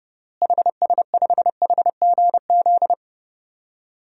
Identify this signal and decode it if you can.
Morse code, heard as a keyed tone, about 30 wpm, 700 Hz. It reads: HS5HGZ
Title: DXCC-100 Call Signs - 6 Characters - Repeated 3x - 30wpm